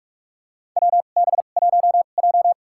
Morse code, heard as a keyed tone, about 43 wpm, spaced as for 32 wpm, 700 Hz. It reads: WB1J
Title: USA Call Signs - 4 Characters - 2x Character Spacing - 45wpm